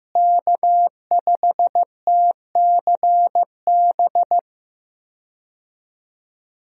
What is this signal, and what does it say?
Morse code, heard as a keyed tone, about 15 wpm, 700 Hz. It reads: K5TCB
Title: USA Call Signs - 5 Characters - Repeated 2x - 15wpm